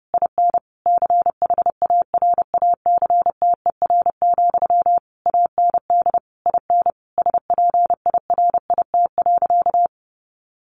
Morse code, heard as a keyed tone, about 30 wpm, 700 Hz. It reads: IN CHARACTER, AND IN SPIRIT.
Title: Sentences from Top 1400 Words 30wpm